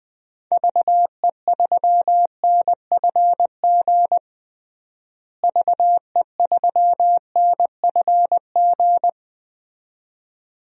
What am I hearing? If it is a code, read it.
Morse code, heard as a keyed tone, about 21 wpm, 700 Hz. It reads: VE3NFG VE3NFG
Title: DXCC-100 Call Signs - 6 Characters - Repeated 2x - 20wpm